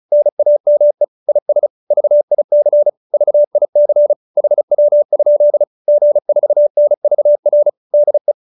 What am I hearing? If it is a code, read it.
Morse code, heard as a keyed tone, about 34 wpm, 590 Hz. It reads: NAME IS VIC VIC HW? G4NVR DE